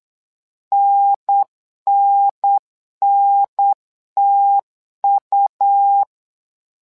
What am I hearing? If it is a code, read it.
Morse code, heard as a keyed tone, about 8 wpm, 800 Hz. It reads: NNNTU
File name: LEZIONI / 2 UHDN / 2 UHDN 8wpm2232